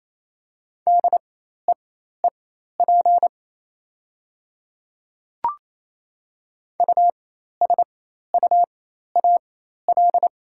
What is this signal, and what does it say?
Morse code, heard as a keyed tone, about 28 wpm, spaced as for 13 wpm, 700 Hz. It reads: DEEP USUAL
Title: 4x Character Spacing - Top 500 Words - 28wpm